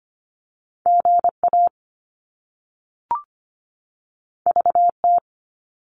Morse code, heard as a keyed tone, about 25 wpm, 700 Hz. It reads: GA VT
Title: US States 25wpm